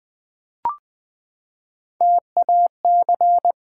Morse code, heard as a keyed tone, about 20 wpm, 700 Hz. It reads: TAC
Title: Build A Word - 3 Letters At A Time - 8-Letter Words - 20wpm